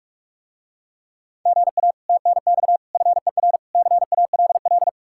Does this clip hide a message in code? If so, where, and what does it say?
Yes: Morse code from 1.5 to 4.9 s, heard as a keyed tone, about 46 wpm, 700 Hz. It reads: GA TNX FER CALL